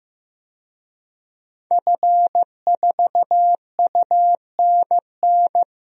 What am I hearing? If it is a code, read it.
Morse code, heard as a keyed tone, about 15 wpm, 700 Hz. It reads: F4UNN
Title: DXCC-100 Call Signs - 5 Characters - Repeated 2x - 15wpm